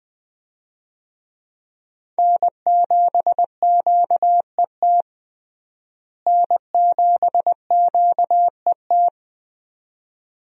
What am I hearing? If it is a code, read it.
Morse code, heard as a keyed tone, about 20 wpm, 700 Hz. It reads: N7QET N7QET
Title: USA Call Signs - 5 Characters - Repeated 2x - 20wpm